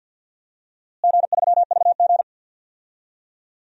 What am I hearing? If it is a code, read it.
Morse code, heard as a keyed tone, about 52 wpm, 700 Hz. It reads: G3VG
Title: DXCC-100 Call Signs - 4 Characters - Repeated 3x - 50wpm